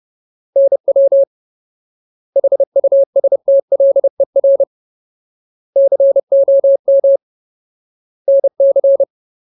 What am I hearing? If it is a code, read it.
Morse code, heard as a keyed tone, about 30 wpm, 550 Hz. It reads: NW HUSTLER COM NC
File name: Ultimate Word List / 30WPM Ultimate